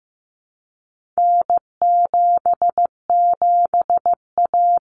Morse code, heard as a keyed tone, about 15 wpm, 700 Hz. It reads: N77A
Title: Common Cut Numbers - 4-Digit 15wpm